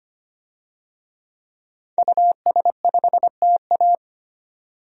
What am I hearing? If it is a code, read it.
Morse code, heard as a keyed tone, about 25 wpm, 700 Hz. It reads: US5TA
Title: DXCC-100 Call Signs - 5 Characters - 25wpm